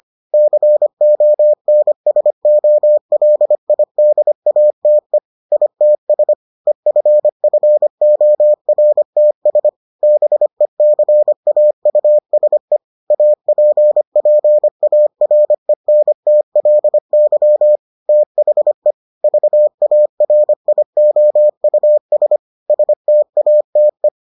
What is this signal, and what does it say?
Morse code, heard as a keyed tone, about 25 wpm, 600 Hz. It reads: CONSOLIDATE ITS EFFORTS BECAUSE APPARENTLY THE VARIOUS STATE